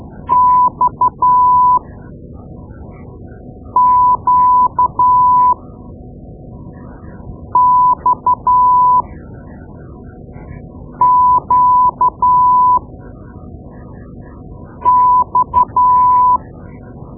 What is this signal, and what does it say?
Morse code, heard as a keyed tone, about 12 wpm, spaced as for 4 wpm, 1000 Hz. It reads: XQXQX
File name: Morse Lessons / Morse Lessons / Tape-2B